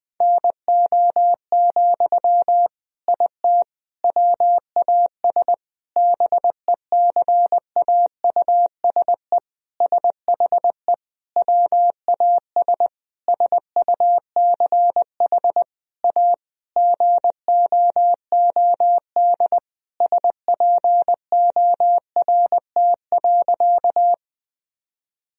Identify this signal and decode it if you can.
Morse code, heard as a keyed tone, about 20 wpm, 700 Hz. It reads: NO, IT WAS BECAUSE SHE WAS SUCH A GOOD SPORT.
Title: Sentences from Top 1000 Words - Review Part 4 - 20wpm